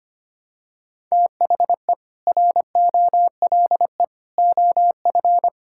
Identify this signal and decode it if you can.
Morse code, heard as a keyed tone, about 25 wpm, 700 Hz. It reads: THE ROLE OF
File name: practice / 25wpm / Sentences with 3-Word Phrases - 25wpm